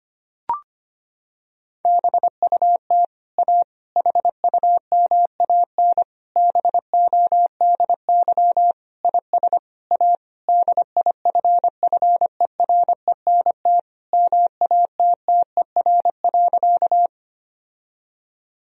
Morse code, heard as a keyed tone, about 25 wpm, 700 Hz. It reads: BUT A HUMAN BODY IS A DIFFERENT MATTER.
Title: Sentences from Top 1000 Words - Review Part 3 - 25wpm